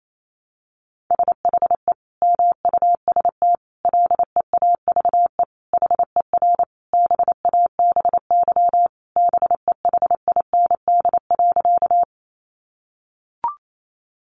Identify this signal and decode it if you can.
Morse code, heard as a keyed tone, about 28 wpm, 700 Hz. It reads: SHE MUST LEAVE HER BABY BEHIND.
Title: Sentences from Top 1300 Words 28wpm